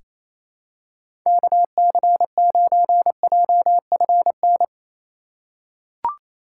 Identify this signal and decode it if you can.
Morse code, heard as a keyed tone, about 28 wpm, 700 Hz. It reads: KC9JFN
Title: USA Call Signs - 2x Character Spacing - 28wpm